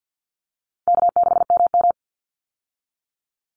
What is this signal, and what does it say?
Morse code, heard as a keyed tone, about 50 wpm, 700 Hz. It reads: K6MM
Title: Easy Call Signs - 50wpm